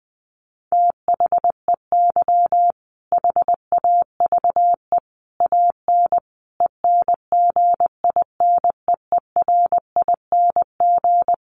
Morse code, heard as a keyed tone, about 19 wpm, 700 Hz. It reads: THEY HAVE AN ENGINEERING
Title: Sentences from Top 1500 Words 20wpm